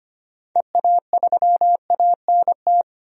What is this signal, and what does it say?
Morse code, heard as a keyed tone, about 25 wpm, 700 Hz. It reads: EA3ANT
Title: DXCC-100 Call Signs - 6 Characters - 25wpm